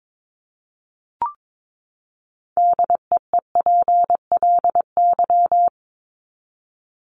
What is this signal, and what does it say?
Morse code, heard as a keyed tone, about 22 wpm, 700 Hz. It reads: DEEPLY
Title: LY Suffix - Top 100 Words with Base Words 22wpm